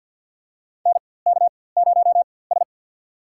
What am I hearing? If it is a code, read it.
Morse code, heard as a keyed tone, about 51 wpm, spaced as for 23 wpm, 700 Hz. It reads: NK0S
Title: USA Call Signs - 4x Character Spacing - 50wpm